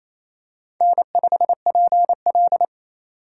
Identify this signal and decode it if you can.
Morse code, heard as a keyed tone, about 28 wpm, 700 Hz. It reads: N5PL
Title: USA Call Signs - 4 Characters - 4x Character Spacing - 28wpm